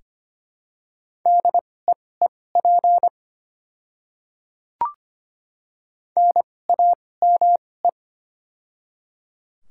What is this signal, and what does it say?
Morse code, heard as a keyed tone, about 25 wpm, spaced as for 18 wpm, 700 Hz. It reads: DEEP NAME
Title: 2x Character Spacing - Top 500 Words - 25wpm